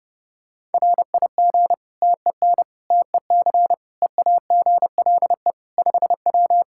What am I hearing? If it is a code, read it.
Morse code, heard as a keyed tone, about 30 wpm, 700 Hz. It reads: RIG TEN TEC EAGLE 5W
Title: Practice QSO - Exchange 2 - Info - 30wpm